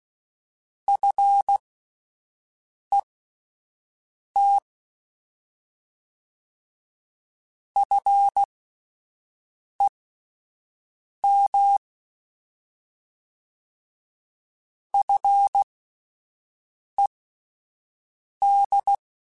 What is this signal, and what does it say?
Morse code, heard as a keyed tone, about 16 wpm, spaced as for 5 wpm, 800 Hz. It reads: FET FEM FED
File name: Alphabetical / Alphabetical mp3s / Fs @5